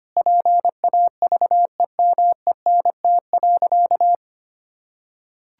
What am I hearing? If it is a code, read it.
Morse code, heard as a keyed tone, about 25 wpm, 700 Hz. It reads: PAVEMENT.
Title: Pangram Challenge v2 - 25wpm